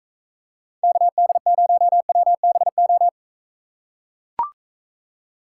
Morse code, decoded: KD0WBO

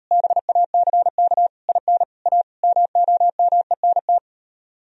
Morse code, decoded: BACK IN A MOMENT

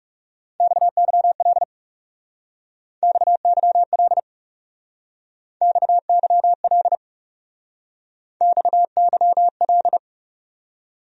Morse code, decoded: XYL XYL XYL XYL

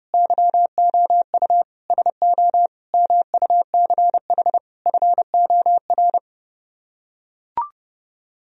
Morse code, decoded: YOU SO MUCH FOR

